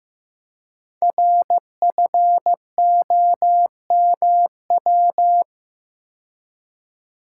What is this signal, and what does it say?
Morse code, heard as a keyed tone, about 15 wpm, 700 Hz. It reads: RFOMW
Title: Code Groups - 5 Letters - 15wpm